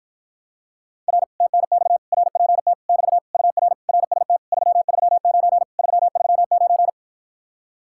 Morse code, heard as a keyed tone, about 53 wpm, 700 Hz. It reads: R TNX RPT <BT> UR RST 339 339